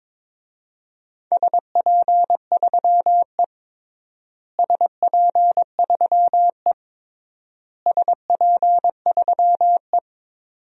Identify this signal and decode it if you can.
Morse code, heard as a keyed tone, about 22 wpm, 700 Hz. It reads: SP3E SP3E SP3E
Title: DXCC-100 Call Signs - 4 Characters - Repeated 3x - 22wpm